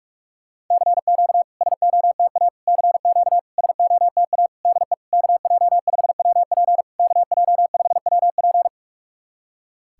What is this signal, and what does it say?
Morse code, heard as a keyed tone, about 41 wpm, 690 Hz. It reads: CQ SOTA CQ SOTA DE KJ5WP KJ5WP
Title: SOTA QSO Practice 45wpm